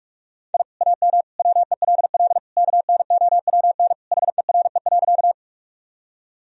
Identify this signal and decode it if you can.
Morse code, heard as a keyed tone, about 42 wpm, 700 Hz. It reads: I AM WELL KNOWN HERE.
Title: Sentences from Top 1000 Words - Review Part 3 - 45wpm